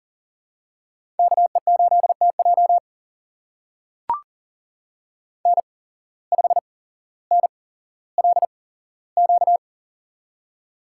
Morse code, heard as a keyed tone, about 40 wpm, spaced as for 17 wpm, 700 Hz. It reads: KE8TJ N5NLQ
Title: USA Call Signs - 5 Characters - 8x Character Spacing - 40wpm